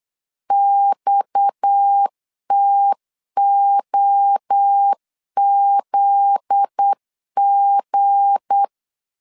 Morse code, heard as a keyed tone, about 8 wpm, 800 Hz. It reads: XTOZG
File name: LEZIONI / 9 2 8wpm 3437